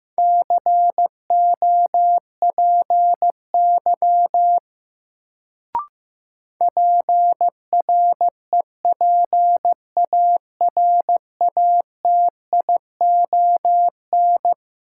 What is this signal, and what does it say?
Morse code, heard as a keyed tone, about 15 wpm, 700 Hz. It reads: COPY PREPARATION
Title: Sentences from Top 1400 Words - New Word Drill Alt - 15wpm